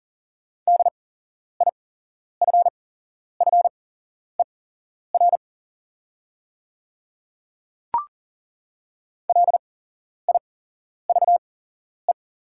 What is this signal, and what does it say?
Morse code, heard as a keyed tone, about 40 wpm, spaced as for 11 wpm, 690 Hz. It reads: DIFFER LIVE
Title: 8x Character Spacing - Top 200 Words - 40wpm